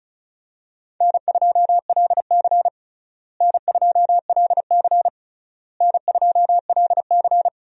Morse code, heard as a keyed tone, about 36 wpm, 700 Hz. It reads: N2LC N2LC N2LC
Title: USA Call Signs - 4 Characters - Repeated 3x - 35wpm